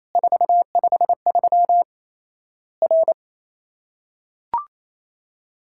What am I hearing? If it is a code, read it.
Morse code, heard as a keyed tone, about 28 wpm, 700 Hz. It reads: 453 R
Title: Worked All Europe DX Contest - QTC Exchange - 28wpm